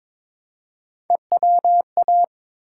Morse code, heard as a keyed tone, about 22 wpm, 700 Hz. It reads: EWA
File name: ARRL Sections v2 - 22wpm